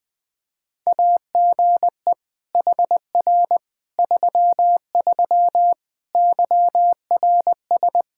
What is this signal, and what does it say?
Morse code, heard as a keyed tone, about 20 wpm, 700 Hz. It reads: AGE HR 33 YRS
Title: Practice QSO - Exchange 3 - More Info - 20wpm